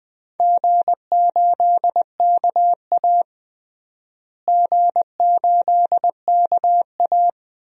Morse code, heard as a keyed tone, about 21 wpm, 700 Hz. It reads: G8KA G8KA
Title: DXCC-100 Call Signs - 4 Characters - Repeated 2x - 20wpm